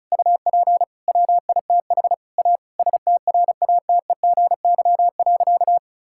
Morse code, heard as a keyed tone, about 36 wpm, 700 Hz. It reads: UP WITH A STRATEGY.
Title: Sentences from Top 1000 Words - Review Part 5 - 35wpm